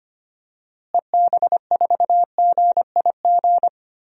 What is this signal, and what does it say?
Morse code, heard as a keyed tone, about 25 wpm, 700 Hz. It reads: EB4GIG
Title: DXCC-100 Call Signs - 6 Characters - Repeated 2x - 25wpm